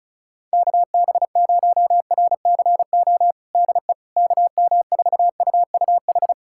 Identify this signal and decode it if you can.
Morse code, heard as a keyed tone, about 36 wpm, 690 Hz. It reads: KB0RCO DE KM4UUH
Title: USA Callsign DE Callsign - 6 Characters - 35wpm